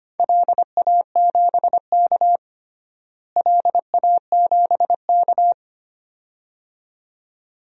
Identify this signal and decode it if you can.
Morse code, heard as a keyed tone, about 25 wpm, 700 Hz. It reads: LA7K LA7K
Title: DXCC-100 Call Signs - 4 Characters - Repeated 2x - 25wpm